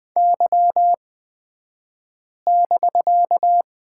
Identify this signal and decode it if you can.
Morse code, heard as a keyed tone, about 20 wpm, spaced as for 5 wpm, 700 Hz. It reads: Y<BK>
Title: 249 Lesson 40d - BK - Two Characters 20wpm